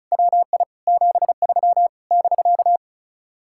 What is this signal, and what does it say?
Morse code, heard as a keyed tone, about 45 wpm, 700 Hz. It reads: WI 73 <BK>